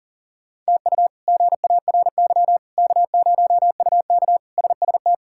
Morse code, heard as a keyed tone, about 37 wpm, 700 Hz. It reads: TU GARY K0UK SST
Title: K1USN SST QSO Practice - 40wpm